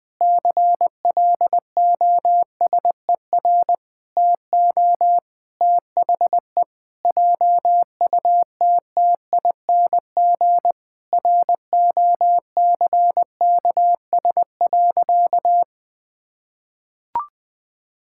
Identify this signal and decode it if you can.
Morse code, heard as a keyed tone, about 20 wpm, 700 Hz. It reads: CLOSER TO THE JUTTING ROCKS.
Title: Pangram Challenge v2 - 20wpm